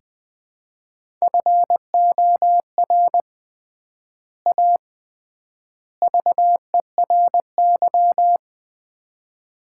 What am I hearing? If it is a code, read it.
Morse code, heard as a keyed tone, about 21 wpm, 700 Hz. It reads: FOR A VERY
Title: 3x Word-Spacing - Sets of 5 Words - 20wpm